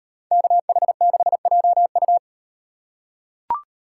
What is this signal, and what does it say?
Morse code, heard as a keyed tone, about 38 wpm, 700 Hz. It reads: KH6JU